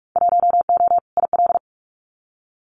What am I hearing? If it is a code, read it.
Morse code, heard as a keyed tone, about 44 wpm, 700 Hz. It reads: 1O IL
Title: Winter Field Day Exchanges 45wpm